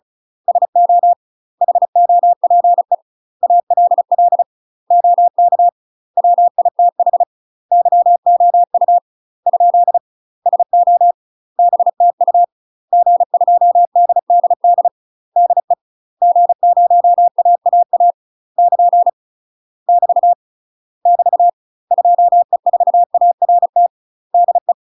Morse code, heard as a keyed tone, about 33 wpm, 700 Hz. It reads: SO HOPE ALL OK WITH YOU ? SO BTU G2DDD DE G0AAA <KN> <BT> <BT> 2E4ART DE
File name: UK MASTER QSOs TRAINING / 35WPM / UK MASTER QSOs 0023